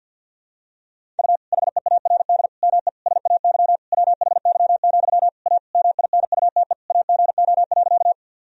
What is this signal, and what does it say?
Morse code, heard as a keyed tone, about 52 wpm, 700 Hz. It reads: U HEARD ME SAY WHY, A MINUTE AGO.
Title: Sentences from Top 500 Words and Abbreviations 50wpm